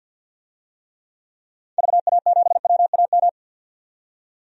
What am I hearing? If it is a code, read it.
Morse code, heard as a keyed tone, about 52 wpm, 700 Hz. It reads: VA7WAM
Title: DXCC-100 Call Signs - 6 Characters - 50wpm